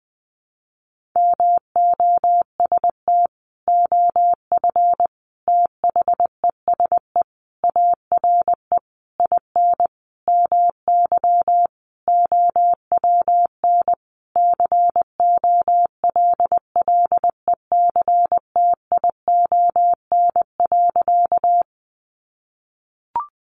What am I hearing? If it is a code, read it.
Morse code, heard as a keyed tone, about 20 wpm, 700 Hz. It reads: MOST OF THESE ARE IN MY OWN COLLECTION.
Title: Sentences from Top 1200 Words 20wpm